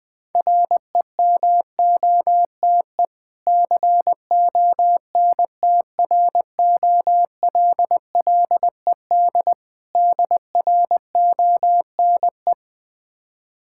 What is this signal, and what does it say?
Morse code, heard as a keyed tone, about 20 wpm, 690 Hz. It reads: REMOTE CONTROLLED DRONE
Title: Difficult Words with Hints - 20wpm